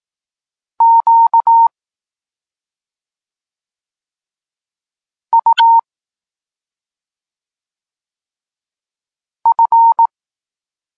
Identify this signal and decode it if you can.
Morse code, heard as a keyed tone, about 18 wpm, 920 Hz. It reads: Q U F
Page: Tekst 32 – Doskonalenie odbioru poznanych liter